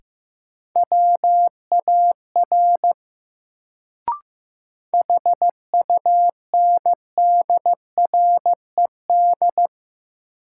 Morse code, heard as a keyed tone, about 15 wpm, 700 Hz. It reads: WAR HUNDRED
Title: Sentences from Top 400 Words - New Word Drill - 15wpm